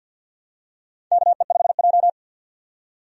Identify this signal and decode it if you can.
Morse code, heard as a keyed tone, about 48 wpm, 700 Hz. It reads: KE5J
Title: USA Call Signs - 4 Characters - 50wpm